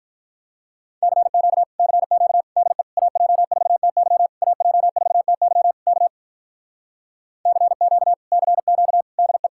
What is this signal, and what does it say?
Morse code, heard as a keyed tone, about 46 wpm, 700 Hz. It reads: CQ CQ DE AJ4TY AJ4TY K CQ CQ DE